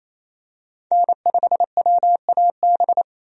Morse code, heard as a keyed tone, about 28 wpm, 700 Hz. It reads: N5WAB
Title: USA Call Signs - 5 Characters - 28wpm